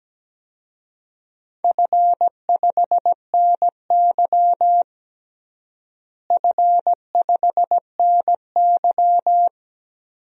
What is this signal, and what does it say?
Morse code, heard as a keyed tone, about 17 wpm, 700 Hz. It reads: F5NY F5NY
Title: DXCC-100 Call Signs - 4 Characters - Repeated 2x - 17wpm